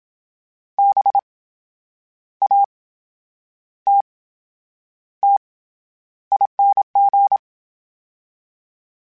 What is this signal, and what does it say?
Morse code, heard as a keyed tone, about 27 wpm, spaced as for 9 wpm, 800 Hz. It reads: BATTING